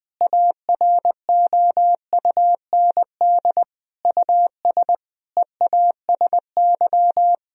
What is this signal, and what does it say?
Morse code, heard as a keyed tone, about 20 wpm, 700 Hz. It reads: AROUND US EASY
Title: Sentences from Top 1000 Words - Review Part 3 - 20wpm